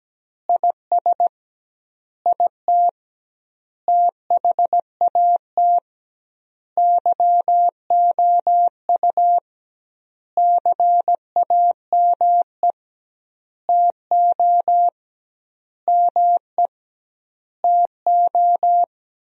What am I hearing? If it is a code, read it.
Morse code, heard as a keyed tone, about 17 wpm, 700 Hz. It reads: IS IT THAT YOU CAME TO ME TO
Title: 2x Word-Spacing - Sentences from Top 200 Words - 17wpm